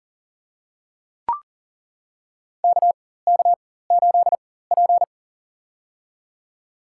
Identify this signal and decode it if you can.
Morse code, heard as a keyed tone, about 40 wpm, spaced as for 19 wpm, 700 Hz. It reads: KK8P
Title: USA Call Signs - 4 Characters - 4x Character Spacing - 40wpm